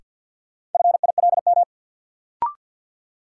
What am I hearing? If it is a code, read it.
Morse code, heard as a keyed tone, about 51 wpm, 700 Hz. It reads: FILM
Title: Sentences from Top 1000 Words - Review Part 3 - New Word Drill - 50wpm